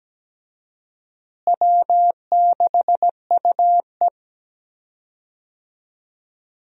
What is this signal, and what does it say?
Morse code, heard as a keyed tone, about 17 wpm, 700 Hz. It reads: W6UE
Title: USA Call Signs - 4 Characters - Repeated 3x - 17wpm